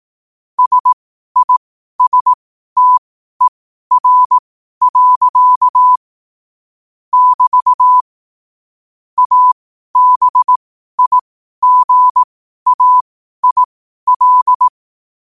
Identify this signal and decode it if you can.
Morse code, heard as a keyed tone, about 18 wpm, spaced as for 12 wpm, 1000 Hz. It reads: SISTER. <BT> ABIGAIL
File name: qotdfiles / 12wpm